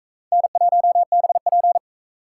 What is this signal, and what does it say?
Morse code, heard as a keyed tone, about 41 wpm, 700 Hz. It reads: N1BP